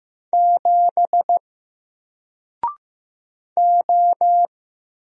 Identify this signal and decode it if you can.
Morse code, heard as a keyed tone, about 15 wpm, 700 Hz. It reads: 7 O
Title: Single Letter-Number Alt 15wpm